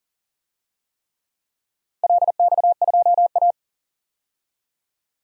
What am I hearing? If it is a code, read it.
Morse code, heard as a keyed tone, about 40 wpm, 700 Hz. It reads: LX2A
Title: DXCC-100 Call Signs - 4 Characters - 40wpm